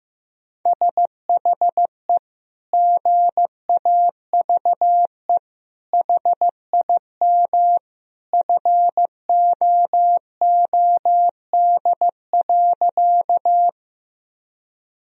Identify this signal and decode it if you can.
Morse code, heard as a keyed tone, about 15 wpm, 700 Hz. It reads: SHE GAVE HIM FOOD.
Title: Sentences from Top 1000 Words - Review Part 1 - 15wpm